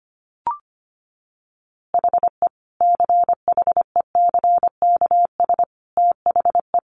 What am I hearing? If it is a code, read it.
Morse code, heard as a keyed tone, about 25 wpm, 700 Hz. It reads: HE CHECKS THE
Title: Easy Phrases 25wpm